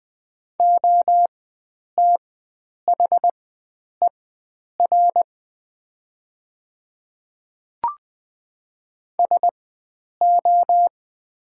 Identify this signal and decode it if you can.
Morse code, heard as a keyed tone, about 20 wpm, spaced as for 9 wpm, 700 Hz. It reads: OTHER SO
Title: 4x Character Spacing - Top 100 Words - 20wpm